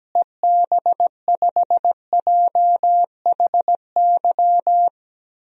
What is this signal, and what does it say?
Morse code, heard as a keyed tone, about 17 wpm, 700 Hz. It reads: EB5JHY
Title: DXCC-100 Call Signs - 6 Characters - Repeated 3x - 17wpm